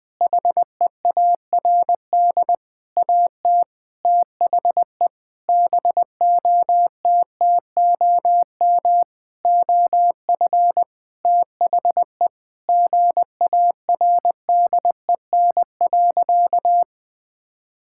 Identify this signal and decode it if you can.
Morse code, heard as a keyed tone, about 19 wpm, 700 Hz. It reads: HEARD AT THE BOTTOM OF THE GARDEN.
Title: Sentences from Top 1000 Words - Review Part 6 - 20wpm